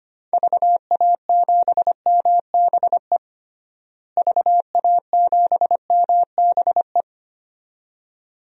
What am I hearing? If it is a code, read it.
Morse code, heard as a keyed tone, about 25 wpm, 700 Hz. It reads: VA7MBE VA7MBE
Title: DXCC-100 Call Signs - 6 Characters - Repeated 2x - 25wpm